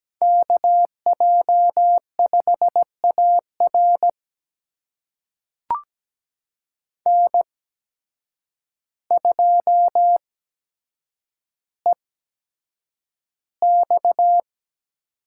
Morse code, heard as a keyed tone, about 17 wpm, spaced as for 8 wpm, 700 Hz. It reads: KJ5AR N2EX
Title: USA Call Signs - 8x Character Spacing - 17wpm